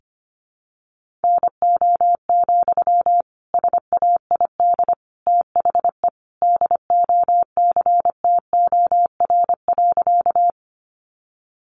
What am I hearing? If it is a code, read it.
Morse code, heard as a keyed tone, about 25 wpm, 700 Hz. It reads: NO, SAID THE DOCTOR.
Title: Sentences from Top 1500 Words 25wpm